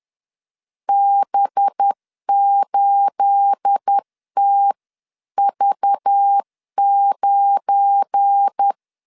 Morse code, heard as a keyed tone, about 10 wpm, 800 Hz. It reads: B8TV9
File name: LEZIONI / 9 28 10wpm 2842